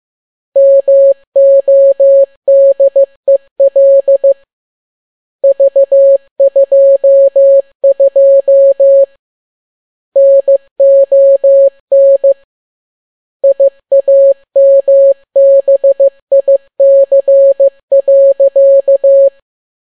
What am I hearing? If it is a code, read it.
Morse code, heard as a keyed tone, about 15 wpm, 550 Hz. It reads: MODEL V22 NON IAMBIC.